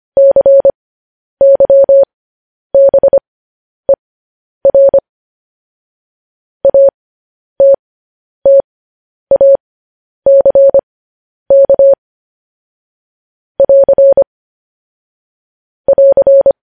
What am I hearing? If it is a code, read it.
Morse code, heard as a keyed tone, about 25 wpm, spaced as for 10 wpm, 550 Hz. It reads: CYBER ATTACK <AR> <AR>
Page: news-headlines-in-morse-code-at-10-wpm